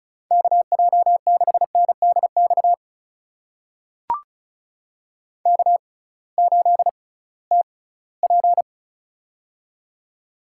Morse code, heard as a keyed tone, about 36 wpm, spaced as for 21 wpm, 700 Hz. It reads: KJ6NDX K8TP
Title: USA Call Signs - 6x Character Spacing - 35wpm